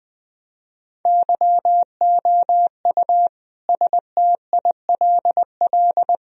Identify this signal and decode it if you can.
Morse code, heard as a keyed tone, about 19 wpm, 700 Hz. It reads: YOU STILL
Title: Sentences from Top 1000 Words - Review Part 1 - 20wpm